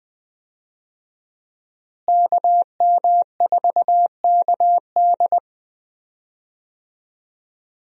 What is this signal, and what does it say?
Morse code, heard as a keyed tone, about 20 wpm, 700 Hz. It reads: KM4KD
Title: USA Call Signs - 5 Characters - Repeated 3x - 20wpm